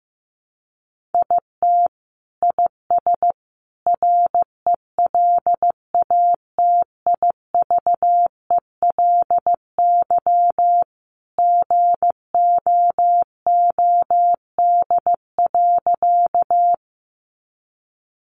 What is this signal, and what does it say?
Morse code, heard as a keyed tone, about 15 wpm, 700 Hz. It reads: IT IS RELATIVELY GOOD.
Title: Sentences from Top 1100 Words 15wpm